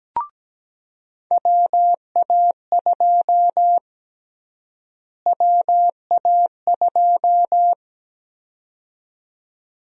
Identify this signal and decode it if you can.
Morse code, heard as a keyed tone, about 17 wpm, 700 Hz. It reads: WA2 WA2
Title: USA Call Sign Prefixes - Repeated 2x - 17wpm